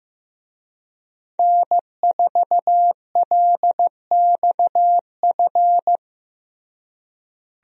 Morse code, heard as a keyed tone, about 15 wpm, 700 Hz. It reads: N4LXF
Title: USA Call Signs - 5 Characters - Repeated 2x - 15wpm